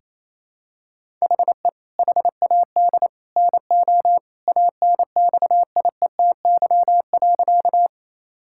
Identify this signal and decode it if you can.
Morse code, heard as a keyed tone, about 28 wpm, 700 Hz. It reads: HE HAD NO ANXIETY.
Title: Sentences from Top 1600 Words 28wpm